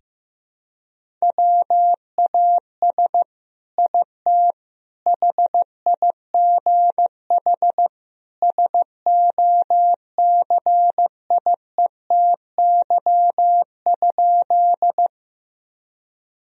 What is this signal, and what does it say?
Morse code, heard as a keyed tone, about 15 wpm, 700 Hz. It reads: WAS IT HIGH SOCIETY?
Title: Sentences from Top 1000 Words - Review Part 4 - 15wpm